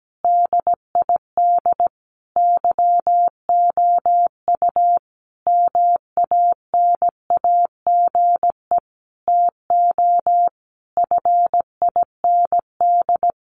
Morse code, heard as a keyed tone, about 17 wpm, 700 Hz. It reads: DID YOU MANAGE TO FIND
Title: Sentences from Top 1400 Words 17wpm